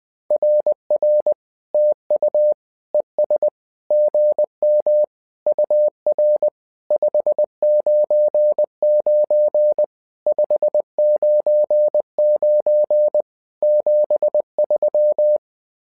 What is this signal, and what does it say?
Morse code, heard as a keyed tone, about 20 wpm, 600 Hz. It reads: RR TU ES GM UR 599 599 73
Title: SOTA QSO Practice 20wpm